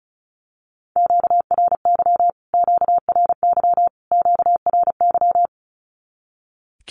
Morse code, heard as a keyed tone, about 34 wpm, 700 Hz. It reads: QRY QRY QRY
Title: Q-Codes Repeated 3x 35wpm